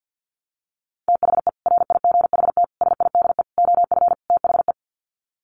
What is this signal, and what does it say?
Morse code, heard as a keyed tone, about 50 wpm, 700 Hz. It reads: THE RIGHT SIDE OF THE
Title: Sets of 5 Words Encore 50wpm